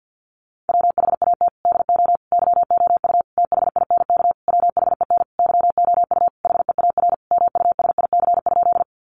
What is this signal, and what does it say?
Morse code, heard as a keyed tone, about 50 wpm, 700 Hz. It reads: WHAT DO YOU THINK WHEN YOU HEAR MUSIC?
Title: Sentences from Top 300 Words 50wpm